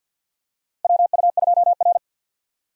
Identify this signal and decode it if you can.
Morse code, heard as a keyed tone, about 52 wpm, 700 Hz. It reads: WU2R
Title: USA Call Signs - 4 Characters - Repeated 2x - 50wpm